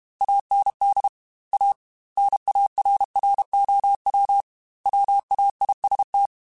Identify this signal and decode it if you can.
Morse code, heard as a keyed tone, about 32 wpm, 800 Hz. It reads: AND A NARROW WAIST